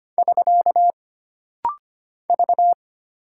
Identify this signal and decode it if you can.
Morse code, heard as a keyed tone, about 25 wpm, 700 Hz. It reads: <SK> V
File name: Single Character (Letter:Number:Punctuation:Prosign) - Rapid-Fire - 25wpm